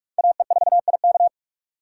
Morse code, heard as a keyed tone, about 47 wpm, 700 Hz. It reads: AE4IK
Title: USA Call Signs - 5 Characters - 6x Character Spacing - 45wpm